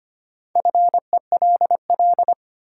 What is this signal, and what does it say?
Morse code, heard as a keyed tone, about 25 wpm, 700 Hz. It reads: FELL
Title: Sentences from Top 1600 Words - New Word Drill Alt - 25wpm